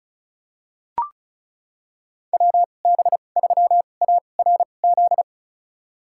Morse code, heard as a keyed tone, about 36 wpm, spaced as for 25 wpm, 700 Hz. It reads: WB3ARZ
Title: USA Call Signs - 6 Characters - 2x Character Spacing - 35wpm